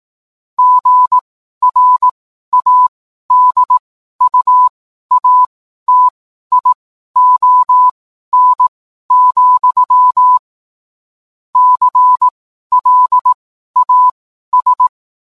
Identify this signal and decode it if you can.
Morse code, heard as a keyed tone, about 18 wpm, spaced as for 12 wpm, 1000 Hz. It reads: GRADUATION, CLAS
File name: cwblog / cw mp3 directory / mem morse 12